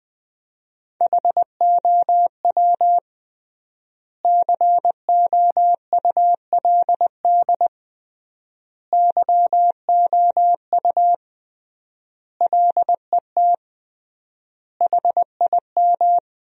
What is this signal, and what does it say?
Morse code, heard as a keyed tone, about 20 wpm, 700 Hz. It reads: HOW COULD YOU LET HIM
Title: 3x Word-Spacing - Sentences from Top 300 Words - 20wpm